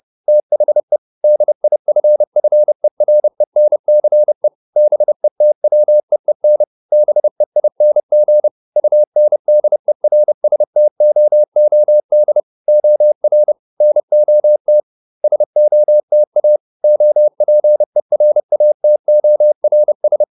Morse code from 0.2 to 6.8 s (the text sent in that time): THE DIFFERENCE BETWEEN